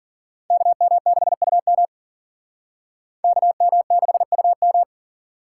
Morse code, heard as a keyed tone, about 51 wpm, 700 Hz. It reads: KM6UM KM6UM